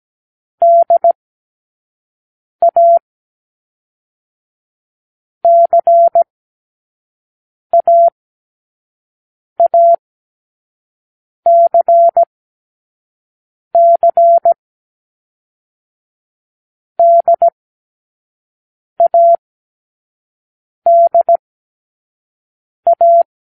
Morse code, heard as a keyed tone, about 17 wpm, 680 Hz. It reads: D A C A A C C D A D A
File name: k7qo / Lesson6(A-D)